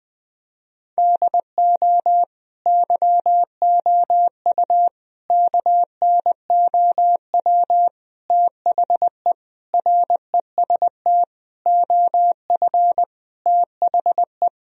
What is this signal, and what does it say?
Morse code, heard as a keyed tone, about 21 wpm, 700 Hz. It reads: DO YOU KNOW THE REST OF THE
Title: Sentences from Top 1000 Words - Review Part 2 - 20wpm